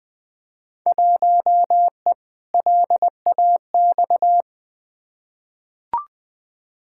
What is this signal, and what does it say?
Morse code, heard as a keyed tone, about 21 wpm, 700 Hz. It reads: 1E LAX
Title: Field Day Exchanges - 20wpm